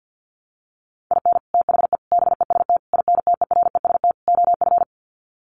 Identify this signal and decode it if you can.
Morse code, heard as a keyed tone, about 50 wpm, 700 Hz. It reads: IN THE BEST INTEREST OF
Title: Sets of 5 Words Encore 50wpm